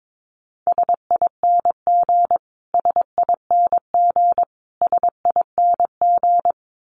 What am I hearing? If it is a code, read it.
Morse code, heard as a keyed tone, about 22 wpm, 700 Hz. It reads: SING SING SING
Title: Top 500 Words Repeated 22wpm